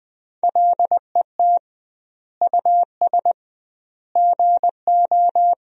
Morse code, heard as a keyed tone, about 21 wpm, 700 Hz. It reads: LET US GO
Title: 2x Word-Spacing - Sentences from Top 300 Words - 20wpm